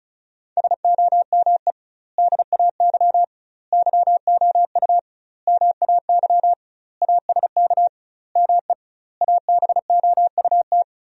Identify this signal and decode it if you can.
Morse code, heard as a keyed tone, about 36 wpm, 700 Hz. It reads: SOME DAY YOU MAY ASK ME ABOUT
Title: 2x Word-Spacing - Sentences from Top 200 Words - 35wpm